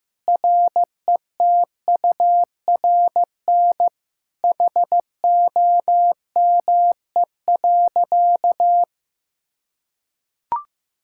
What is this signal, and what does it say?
Morse code, heard as a keyed tone, about 16 wpm, 700 Hz. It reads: RETURN HOME.
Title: Sentences from Top 1000 Words - Review Part 5 - 15wpm